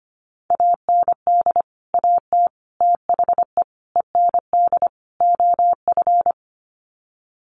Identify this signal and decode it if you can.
Morse code, heard as a keyed tone, about 25 wpm, 700 Hz. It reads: AND AT THE END OF
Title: Sets of 5 Words 25wpm